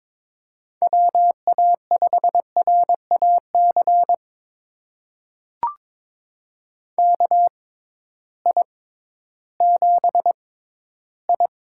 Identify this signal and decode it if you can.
Morse code, heard as a keyed tone, about 23 wpm, spaced as for 13 wpm, 700 Hz. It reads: WA5RAC KI7I
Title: USA Call Signs - 6 Characters - 6x Character Spacing - 22wpm